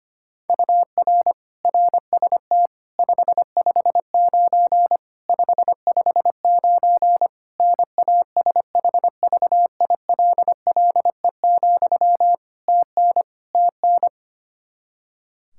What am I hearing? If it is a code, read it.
Morse code, heard as a keyed tone, about 25 wpm, 700 Hz. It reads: UR RST 559 559 NASHVILLE, TN TN
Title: Practice QSO - Exchange 1 - Beginning - 25wpm